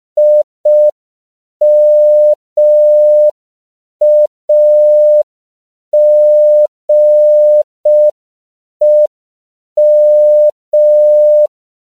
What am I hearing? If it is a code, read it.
Morse code, heard as a keyed tone, about 5 wpm, 600 Hz. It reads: IMAGEM
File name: CW / 5PPM